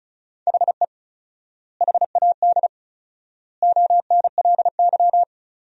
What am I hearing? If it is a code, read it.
Morse code, heard as a keyed tone, about 36 wpm, 700 Hz. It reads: HE HAD ONLY
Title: 4x Word-Spacing - Sentences from Top 300 Words - 35wpm